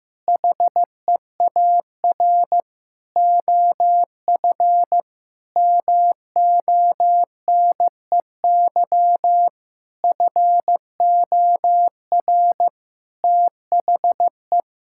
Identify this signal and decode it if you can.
Morse code, heard as a keyed tone, about 16 wpm, 700 Hz. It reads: HEAR OF MONEY FOR THE
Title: Sentences from Top 1000 Words - Review Part 5 - 15wpm